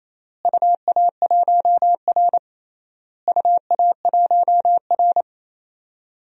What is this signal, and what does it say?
Morse code, heard as a keyed tone, about 28 wpm, 700 Hz. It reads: UA1R UA1R
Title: DXCC-100 Call Signs - 4 Characters - Repeated 3x - 28wpm